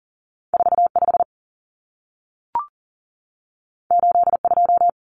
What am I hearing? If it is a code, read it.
Morse code, heard as a keyed tone, about 40 wpm, 700 Hz. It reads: 45 82
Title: Common Cut Numbers - 2-Digit 40wpm